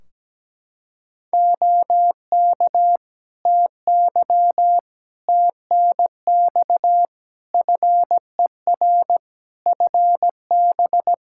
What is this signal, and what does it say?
Morse code, heard as a keyed tone, about 17 wpm, 700 Hz. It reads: OK TY TNX FER FB
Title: K1USN SST QSO Practice 17wpm